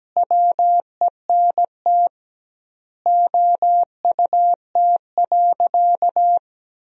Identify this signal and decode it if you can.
Morse code, heard as a keyed tone, about 17 wpm, 700 Hz. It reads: WENT OUT.
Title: 2x Word-Spacing - Sentences from Top 300 Words - 17wpm